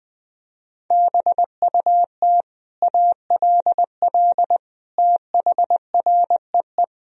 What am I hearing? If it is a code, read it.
Morse code, heard as a keyed tone, about 21 wpm, 700 Hz. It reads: BUT ALL THREE
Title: Sentences from Top 1000 Words - Review Part 3 - 20wpm